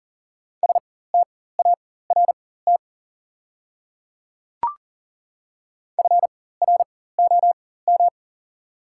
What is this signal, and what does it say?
Morse code, heard as a keyed tone, about 37 wpm, spaced as for 18 wpm, 690 Hz. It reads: START FROM
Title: 4x Character Spacing - Top 500 Words - 40wpm